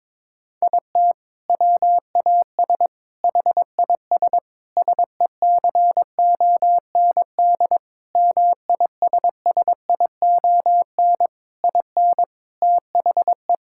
Morse code, heard as a keyed tone, about 22 wpm, 700 Hz. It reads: IT WAS HIS SECOND MISSION IN THE
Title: Sentences from Top 1600 Words 22wpm